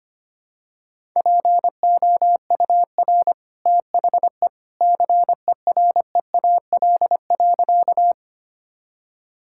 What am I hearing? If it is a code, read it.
Morse code, heard as a keyed tone, about 25 wpm, 700 Hz. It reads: POUR THE CEREAL.
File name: Easy Sentences with Hints - 25wpm